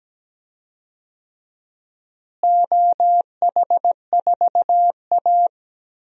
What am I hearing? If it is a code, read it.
Morse code, heard as a keyed tone, about 17 wpm, 700 Hz. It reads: OH4A
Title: DXCC-100 Call Signs - 4 Characters - Repeated 3x - 17wpm